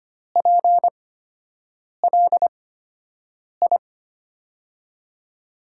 Morse code, heard as a keyed tone, about 25 wpm, spaced as for 7 wpm, 700 Hz. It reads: PLI
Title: USA Call Signs - 5 Characters - 8x Character Spacing - 25wpm